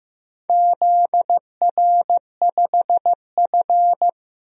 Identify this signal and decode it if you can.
Morse code, heard as a keyed tone, about 15 wpm, 700 Hz. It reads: ZR5F
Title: DXCC-100 Call Signs - 4 Characters - Repeated 2x - 15wpm